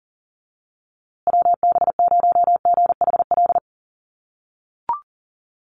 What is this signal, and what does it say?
Morse code, heard as a keyed tone, about 40 wpm, 700 Hz. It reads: WB0GHL
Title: USA Call Signs Repeated 2x 40wpm